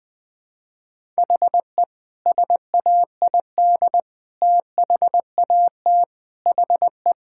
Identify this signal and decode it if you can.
Morse code, heard as a keyed tone, about 20 wpm, 700 Hz. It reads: HE SAID THAT HE
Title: Sentences with 5-Word Phrases - Spoken Beginning - 20wpm